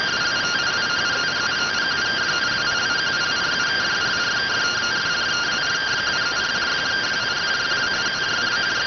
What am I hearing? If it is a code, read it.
Two-tone teleprinter signal, about 170 Hz shift.